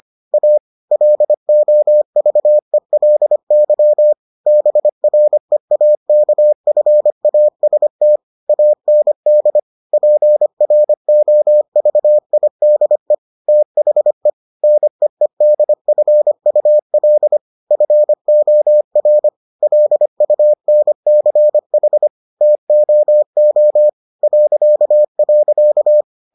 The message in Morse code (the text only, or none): A LOVELY BREAKFAST AND PROVIDE THE NEEDFUL FOR LUNCH TOO ..